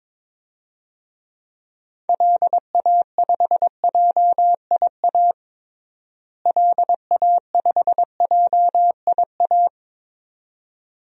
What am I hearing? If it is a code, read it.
Morse code, heard as a keyed tone, about 22 wpm, 700 Hz. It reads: LA5JIA LA5JIA
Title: DXCC-100 Call Signs - 6 Characters - Repeated 2x - 22wpm